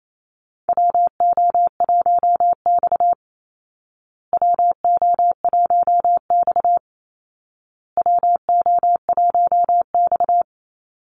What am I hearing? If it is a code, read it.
Morse code, heard as a keyed tone, about 28 wpm, 700 Hz. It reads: WO1X WO1X WO1X
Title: Call Signs Repeated 2x - 28wpm